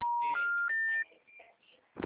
Special information tone.